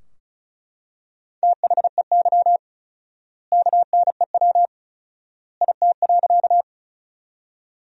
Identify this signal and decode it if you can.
Morse code, heard as a keyed tone, about 36 wpm, 700 Hz. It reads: THEY KNEW IT.
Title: 4x Word-Spacing - Sentences from Top 400 Words - 35wpm